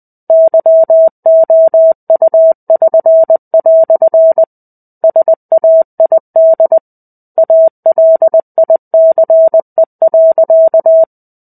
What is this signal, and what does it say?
Morse code, heard as a keyed tone, about 20 wpm, 650 Hz. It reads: YOU<SN>" SAID ALICE.
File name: 05-Chapter-05 20 WPM